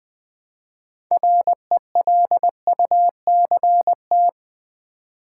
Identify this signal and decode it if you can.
Morse code, heard as a keyed tone, about 20 wpm, 700 Hz. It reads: RELUCT
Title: Build A Word - 3 Letters At A Time - 9-Letter Words - 20wpm